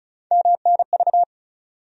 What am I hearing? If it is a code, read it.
Morse code, heard as a keyed tone, about 34 wpm, 700 Hz. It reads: MNV
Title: USA Call Sign Postfixes - 3 Letters 35wpm